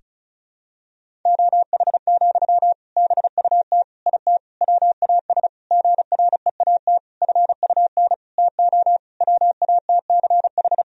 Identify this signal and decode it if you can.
Morse code, heard as a keyed tone, about 34 wpm, 700 Hz. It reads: OH, BUT IT WAS GREAT FUN TO WATCH
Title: Sentences from Top 1000 Words - Review Part 4 - 35wpm